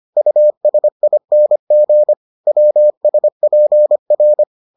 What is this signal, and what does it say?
Morse code, heard as a keyed tone, about 25 wpm, 600 Hz. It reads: USING WSPR